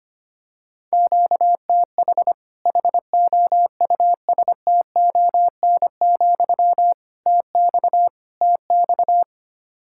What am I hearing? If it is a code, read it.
Morse code, heard as a keyed tone, about 25 wpm, 700 Hz. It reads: QTH HOUSTON, TX TX
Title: Practice QSO - Exchange 1 - Beginning - 25wpm